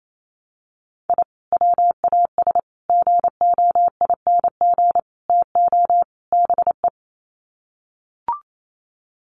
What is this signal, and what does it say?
Morse code, heard as a keyed tone, about 28 wpm, 700 Hz. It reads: I WAS GOING TO BE
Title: Sets of 5 Words 28wpm